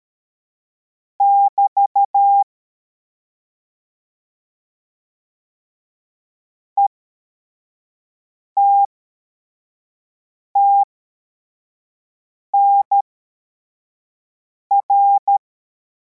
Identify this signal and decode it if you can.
Morse code, heard as a keyed tone, about 13 wpm, spaced as for 4 wpm, 800 Hz. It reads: <BT> ETTNR